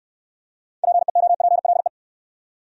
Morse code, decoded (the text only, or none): PEOPLE